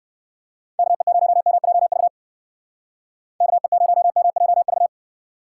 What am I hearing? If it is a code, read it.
Morse code, heard as a keyed tone, about 58 wpm, 700 Hz. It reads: KE0MJV KE0MJV